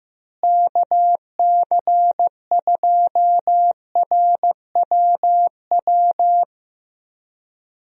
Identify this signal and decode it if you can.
Morse code, heard as a keyed tone, about 15 wpm, 700 Hz. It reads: KC2RWW
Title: USA Call Signs - 6 Characters - Repeated 2x - 15wpm